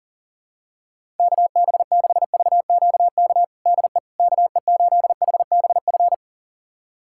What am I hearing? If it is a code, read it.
Morse code, heard as a keyed tone, about 40 wpm, 700 Hz. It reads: KB6VQK DE KE8HBF
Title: USA Callsign DE Callsign - 6 Characters - 40wpm